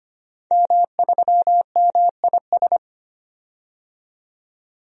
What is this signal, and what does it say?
Morse code, heard as a keyed tone, about 25 wpm, 700 Hz. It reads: M3MIS
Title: DXCC-100 Call Signs - 5 Characters - 25wpm